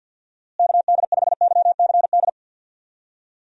Morse code, heard as a keyed tone, about 52 wpm, 700 Hz. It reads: KD5YCD
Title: USA Call Signs - 6 Characters - Repeated 2x - 50wpm